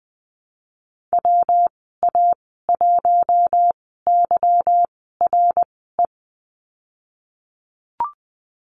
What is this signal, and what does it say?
Morse code, heard as a keyed tone, about 20 wpm, spaced as for 14 wpm, 700 Hz. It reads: WA1YRE
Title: 186 Lesson 30g - Letter b - Call Signs 20wpm